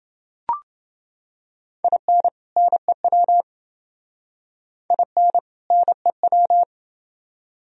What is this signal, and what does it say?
Morse code, heard as a keyed tone, about 28 wpm, 700 Hz. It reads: IN NEW IN NEW